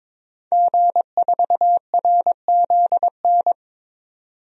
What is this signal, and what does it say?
Morse code, heard as a keyed tone, about 21 wpm, 700 Hz. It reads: G4RZN
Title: DXCC-100 Call Signs - 5 Characters - 22wpm